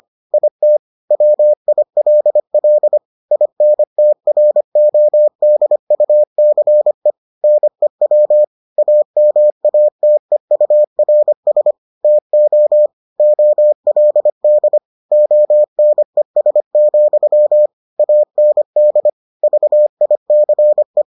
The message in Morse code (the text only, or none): IT WILL INTRODUCE NEW AMATEURS TO OLD ONES, AND VICE